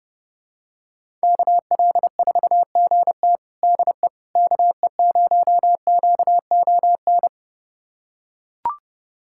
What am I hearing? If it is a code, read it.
Morse code, heard as a keyed tone, about 30 wpm, 700 Hz. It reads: KL4GT DE KE0QON
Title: USA Callsign DE Callsign 30wpm